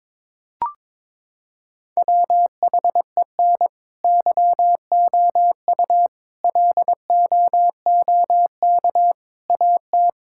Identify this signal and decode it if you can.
Morse code, heard as a keyed tone, about 22 wpm, 700 Hz. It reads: WHEN YOU LOOK AT
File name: practice / 22wpm / Sentences with 3-Word Phrases - 22wpm